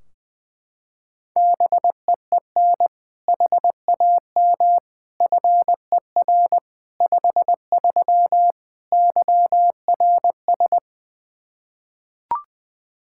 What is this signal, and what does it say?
Morse code, heard as a keyed tone, about 21 wpm, 700 Hz. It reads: BEEN HAM FER 53 YRS
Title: Practice QSO - Exchange 3 - More Info - 20wpm